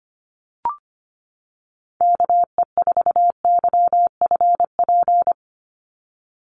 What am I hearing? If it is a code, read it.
Morse code, heard as a keyed tone, about 25 wpm, 700 Hz. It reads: KE4YFP
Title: USA Call Signs 25wpm